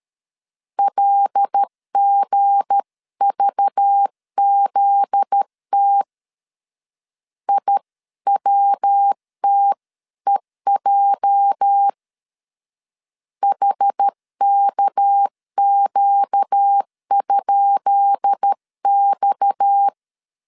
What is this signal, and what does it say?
Morse code, heard as a keyed tone, about 13 wpm, 800 Hz. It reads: LGVZT IWTEJ HKQ?X
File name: LEZIONI / 10 SIGNS / 10B INT 12wpm 2414